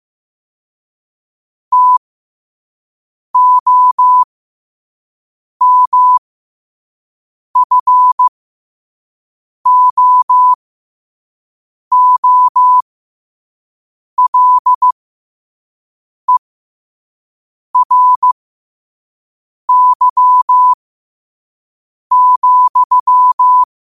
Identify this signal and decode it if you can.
Morse code, heard as a keyed tone, about 15 wpm, spaced as for 5 wpm, 1000 Hz. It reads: TOMFOOLERY,
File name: qotdfiles / 5wpm